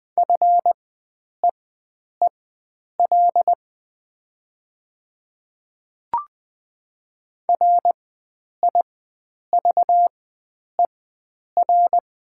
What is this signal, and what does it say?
Morse code, heard as a keyed tone, about 20 wpm, spaced as for 9 wpm, 700 Hz. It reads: FEEL RIVER
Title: 4x Character Spacing - Top 500 Words - 20wpm